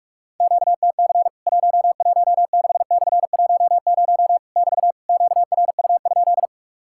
Morse code, heard as a keyed tone, about 44 wpm, 700 Hz. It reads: QTC 116/10 <BT> QRU?
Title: Worked All Europe DX Contest - QTC Exchange - 45wpm